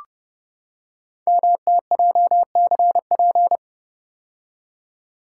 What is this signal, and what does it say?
Morse code, heard as a keyed tone, about 30 wpm, 700 Hz. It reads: MTJCP
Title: Code Groups - 5 Letters - 30wpm